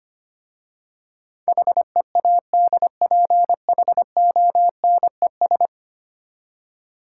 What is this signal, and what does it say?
Morse code, heard as a keyed tone, about 25 wpm, 700 Hz. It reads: HEADPHONES
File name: practice / 25wpm / Easy Words with Hints - 25wpm